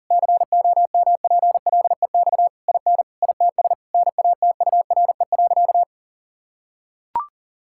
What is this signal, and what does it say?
Morse code, heard as a keyed tone, about 40 wpm, 700 Hz. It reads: COMPLEX IN ITS NATURE.
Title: Sentences from Top 1000 Words - Review Part 6 - 40wpm